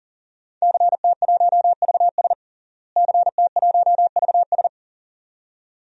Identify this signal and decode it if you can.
Morse code, heard as a keyed tone, about 40 wpm, 700 Hz. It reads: CT1VS CT1VS
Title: DXCC-100 Call Signs - 5 Characters - Repeated 2x - 40wpm